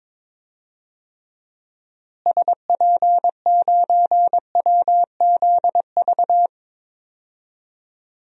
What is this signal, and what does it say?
Morse code, heard as a keyed tone, about 21 wpm, 700 Hz. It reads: SP9WZV
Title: DXCC-100 Call Signs - 6 Characters - Repeated 2x - 22wpm